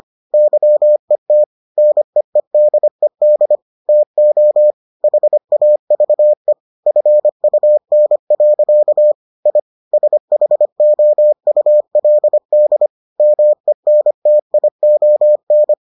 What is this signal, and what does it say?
Morse code, heard as a keyed tone, about 25 wpm, 600 Hz. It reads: YET NEEDED TO HAVE FUN. I SHOULD MENTION